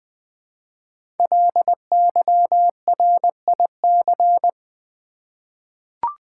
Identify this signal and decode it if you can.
Morse code, heard as a keyed tone, about 21 wpm, 700 Hz. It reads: LYRIC
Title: 5 Letter Words - First 3 Letters Given - 20wpm